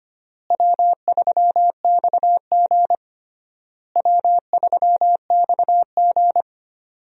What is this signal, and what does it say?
Morse code, heard as a keyed tone, about 25 wpm, 700 Hz. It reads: W3XG W3XG
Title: USA Call Signs - 4 Characters - Repeated 3x - 25wpm